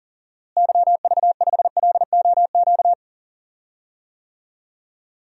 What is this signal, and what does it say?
Morse code, heard as a keyed tone, about 37 wpm, 700 Hz. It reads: YV5LOQ
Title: DXCC-100 Call Signs - 6 Characters - Repeated 2x - 40wpm